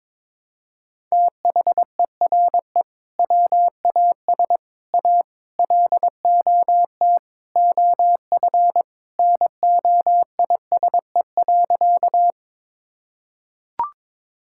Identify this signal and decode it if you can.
Morse code, heard as a keyed tone, about 22 wpm, 700 Hz. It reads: THERE WAS A LOT OF NOISE.
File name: practice / 22wpm / Sentences with 4-Word Phrases - 22wpm